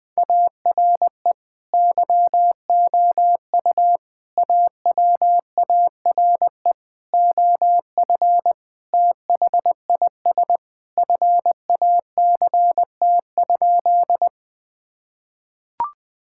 Morse code, heard as a keyed tone, about 20 wpm, 700 Hz. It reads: ARE YOU AWARE OF THIS FACT?
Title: Sentences with 2-Word Phrases - Spoken Beginning - 20wpm